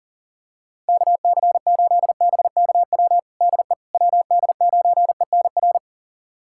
Morse code, heard as a keyed tone, about 40 wpm, 690 Hz. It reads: KC8BKW DE WD9ENR
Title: USA Callsign DE Callsign - 6 Characters - 40wpm